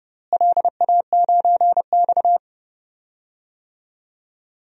Morse code, heard as a keyed tone, about 31 wpm, 700 Hz. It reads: LA9X
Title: DXCC-100 Call Signs - 4 Characters - Repeated 2x - 30wpm